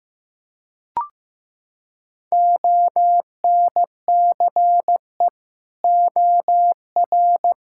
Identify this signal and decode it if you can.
Morse code, heard as a keyed tone, about 15 wpm, 700 Hz. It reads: ONCE OR
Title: Sentences from Top 1600 Words 15wpm